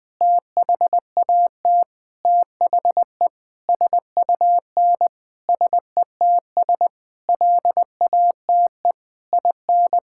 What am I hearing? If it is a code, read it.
Morse code, heard as a keyed tone, about 20 wpm, 700 Hz. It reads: THAT THE SUN SETS LATE IN
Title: Sentences with 8-Word Phrases - Spoken Beginning - 20wpm